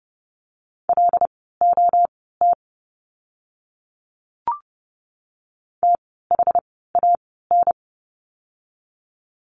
Morse code, heard as a keyed tone, about 31 wpm, spaced as for 17 wpm, 700 Hz. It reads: LOT THAN
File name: Lesson 16f - Letter c - Words 30wpm